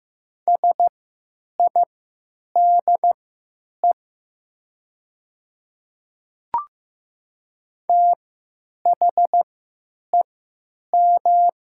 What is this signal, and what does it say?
Morse code, heard as a keyed tone, about 16 wpm, spaced as for 9 wpm, 700 Hz. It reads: SIDE THEM
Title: 3x Character Spacing - Top 300 Words - 15wpm